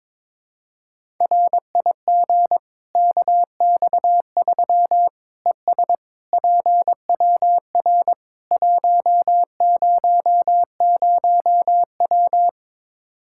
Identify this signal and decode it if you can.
Morse code, heard as a keyed tone, about 21 wpm, 700 Hz. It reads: RIG KX3 ES PWR 100W
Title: K1USN SST QSO Practice 22wpm